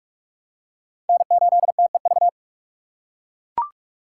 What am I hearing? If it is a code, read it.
Morse code, heard as a keyed tone, about 42 wpm, 700 Hz. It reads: N8TEV
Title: USA Call Signs - 4x Character Spacing - 45wpm